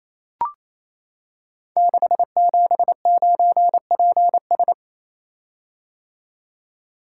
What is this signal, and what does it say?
Morse code, heard as a keyed tone, about 27 wpm, 700 Hz. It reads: 679PS
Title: Code Groups - 5 Letter:Numbers - 28wpm